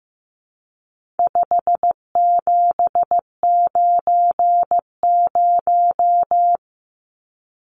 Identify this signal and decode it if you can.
Morse code, heard as a keyed tone, about 15 wpm, 700 Hz. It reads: 5790
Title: Numbers - 4-Digit 15wpm